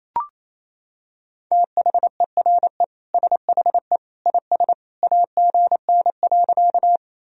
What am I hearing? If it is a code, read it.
Morse code, heard as a keyed tone, about 28 wpm, 700 Hz. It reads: THERE SHE IS AGN.
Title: Sentences from Top 200 Words and Abbreviations 28wpm